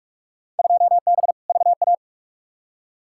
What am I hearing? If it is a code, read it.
Morse code, heard as a keyed tone, about 43 wpm, 700 Hz. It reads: 2B VA
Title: Field Day Exchanges Encore - 45wpm